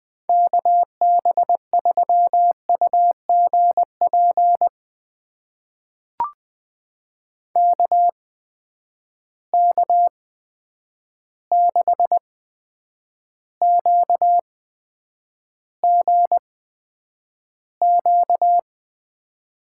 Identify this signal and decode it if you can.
Morse code, heard as a keyed tone, about 20 wpm, spaced as for 9 wpm, 700 Hz. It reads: KB3UGP KK6QGQ
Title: USA Call Signs - 6 Characters - 8x Character Spacing - 20wpm